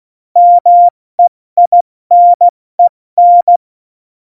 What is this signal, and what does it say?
Morse code, heard as a keyed tone, about 16 wpm, spaced as for 14 wpm, 700 Hz. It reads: MEINEN
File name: Morsekurs / Witze / Witz 05. September 17, Tempo 70